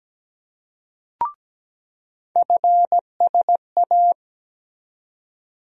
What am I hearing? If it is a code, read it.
Morse code, heard as a keyed tone, about 17 wpm, 700 Hz. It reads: FSA
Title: USA Call Sign Postfixes - 3 Letters 17wpm